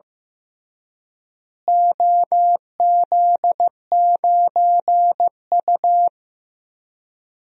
Morse code, heard as a keyed tone, about 15 wpm, 700 Hz. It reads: OZ9U
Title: DXCC-100 Call Signs - 4 Characters - Repeated 3x - 15wpm